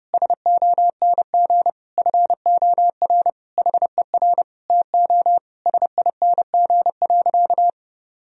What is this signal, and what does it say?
Morse code, heard as a keyed tone, about 32 wpm, 700 Hz. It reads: SONG FOR HER TO SING.
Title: Sentences from Top 1000 Words - Review Part 2 - 30wpm